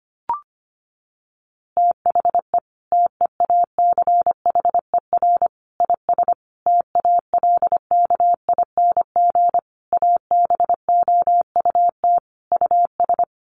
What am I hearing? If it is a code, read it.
Morse code, heard as a keyed tone, about 25 wpm, 700 Hz. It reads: THE TEACHER IS TALKING ABOUT US
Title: Easy Phrases 25wpm